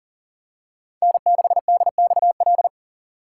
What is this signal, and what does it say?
Morse code, heard as a keyed tone, about 40 wpm, 700 Hz. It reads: N6DXL
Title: USA Call Signs - 5 Characters - 3x Character Spacing - 40wpm